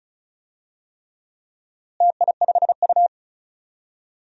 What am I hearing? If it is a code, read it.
Morse code, heard as a keyed tone, about 34 wpm, 700 Hz. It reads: TI5U
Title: DXCC-100 Call Signs - 4 Characters - Repeated 3x - 35wpm